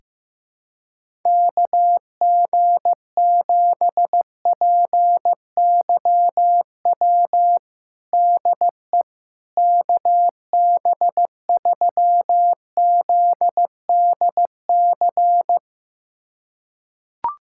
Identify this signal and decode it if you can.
Morse code, heard as a keyed tone, about 15 wpm, 700 Hz. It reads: KG7PYW DE KB3ZDC
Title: USA Callsign DE Callsign - 6 Characters - 15wpm